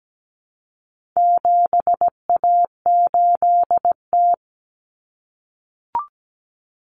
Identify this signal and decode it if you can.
Morse code, heard as a keyed tone, about 17 wpm, 700 Hz. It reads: 7A8T
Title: Common Cut Numbers - 4-Digit 17wpm